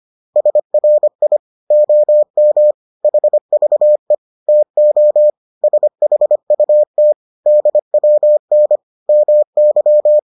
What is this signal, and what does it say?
Morse code, heard as a keyed tone, about 25 wpm, 600 Hz. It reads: SRI OM HVE TO SHUT DWN MY